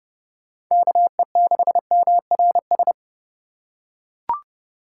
Morse code, heard as a keyed tone, about 30 wpm, 700 Hz. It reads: KE6MRS
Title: USA Call Signs - 6 Characters - 2x Character Spacing - 30wpm